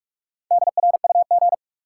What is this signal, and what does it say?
Morse code, heard as a keyed tone, about 41 wpm, 700 Hz. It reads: DRUG
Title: Sets of 5 Words - New Word Drill - 45wpm